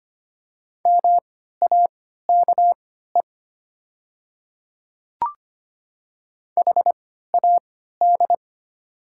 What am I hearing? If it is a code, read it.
Morse code, heard as a keyed tone, about 25 wpm, spaced as for 14 wpm, 700 Hz. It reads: MAKE HAD
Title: 3x Character Spacing - Top 100 Words - 25wpm